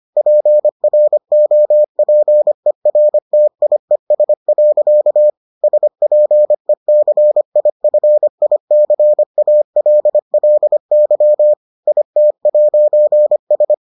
Morse code, heard as a keyed tone, about 25 wpm, 600 Hz. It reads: PROPERTIES. SPECIFICALLY IT'S